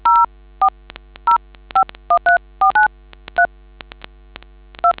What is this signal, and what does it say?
DTMF, heard as telephone touch tones, tones of about 110 ms. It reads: 040513493 2